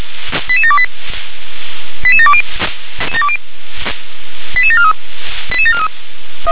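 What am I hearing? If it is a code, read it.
Five-tone selective call (ZVEI), heard as quick runs of five tones, tones of about 70 ms. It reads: 88728 88720 88743 88743
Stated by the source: ZVEI five-tone selective call received with an SDR.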